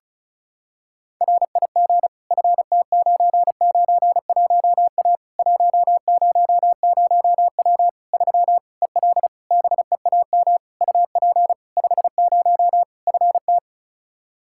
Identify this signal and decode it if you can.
Morse code, heard as a keyed tone, about 36 wpm, 700 Hz. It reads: RIG FT991A 100W 3 EL BEAM UP 50 FT
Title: K1USN SST QSO Practice 35wpm